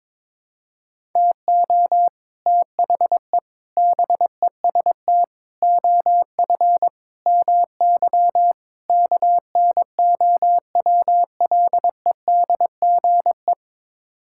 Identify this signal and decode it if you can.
Morse code, heard as a keyed tone, about 21 wpm, 700 Hz. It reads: TO THE BEST OF MY KNOWLEDGE
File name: practice / 22wpm / Sets of 6 Words - 22wpm